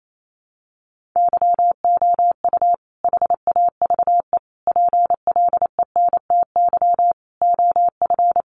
Morse code, heard as a keyed tone, about 28 wpm, 700 Hz. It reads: YOU HAVE PLENTY OF
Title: Sentences from Top 1300 Words 28wpm